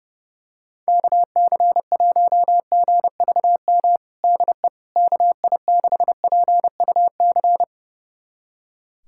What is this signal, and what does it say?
Morse code, heard as a keyed tone, about 30 wpm, 700 Hz. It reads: KC1GVM DE KI6PUC
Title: USA Callsign DE Callsign 30wpm